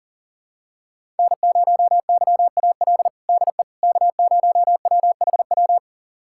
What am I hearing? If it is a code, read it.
Morse code, heard as a keyed tone, about 40 wpm, 700 Hz. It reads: N0YAL DE K0WHW
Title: USA Callsign DE Callsign - 5 Characters - 40wpm